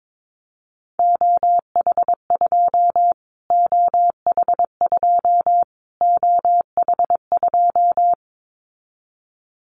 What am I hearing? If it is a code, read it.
Morse code, heard as a keyed tone, about 22 wpm, 700 Hz. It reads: OH2 OH2 OH2
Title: Callsign Prefixes for Top 25 DXCC - Repeated 3x - 22wpm